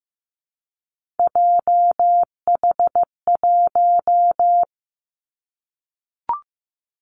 Morse code, heard as a keyed tone, about 15 wpm, 700 Hz. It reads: JH1
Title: Callsign Prefixes for Top 25 DXCC - Repeated 3x - 15wpm